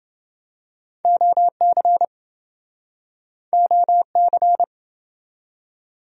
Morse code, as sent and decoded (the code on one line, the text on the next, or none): --- -.-. / --- -.-.
OC OC